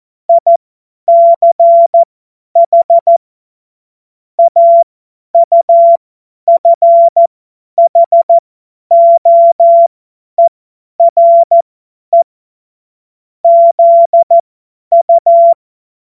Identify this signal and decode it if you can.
Morse code, heard as a keyed tone, about 14 wpm, spaced as for 10 wpm, 680 Hz. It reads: ICH AUFHOERE ZU
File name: Morsekurs / Witze / Witz 31. Januar 2023 50 F70